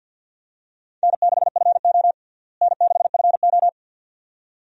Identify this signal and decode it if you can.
Morse code, heard as a keyed tone, about 47 wpm, 700 Hz. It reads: N6FO N6FO
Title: USA Call Signs - 4 Characters - Repeated 2x - 50wpm